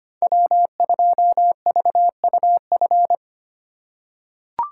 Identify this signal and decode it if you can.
Morse code, heard as a keyed tone, about 25 wpm, 700 Hz. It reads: W2VUF
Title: K1USN SST QSO Practice - 25wpm